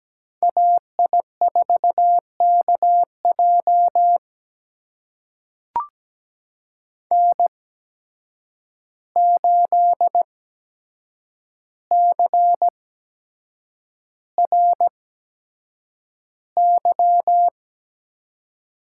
Morse code, heard as a keyed tone, about 17 wpm, spaced as for 7 wpm, 700 Hz. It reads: AI4KJ N8CRY
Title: USA Call Signs - 5 Characters - 8x Character Spacing - 17wpm